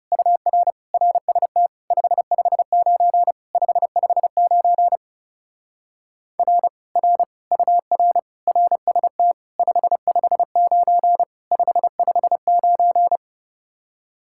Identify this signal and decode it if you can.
Morse code, heard as a keyed tone, about 37 wpm, 700 Hz. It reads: UR RST 559 559 R R UR RST 559 559
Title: Speed-Racing Common QSO Phrases Encore - 30wpm